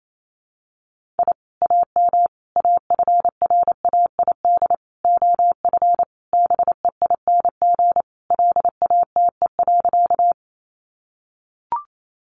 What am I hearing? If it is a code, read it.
Morse code, heard as a keyed tone, about 28 wpm, 700 Hz. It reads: I AM AFRAID OF BEING LATE.
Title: Sentences from Top 1500 Words 28wpm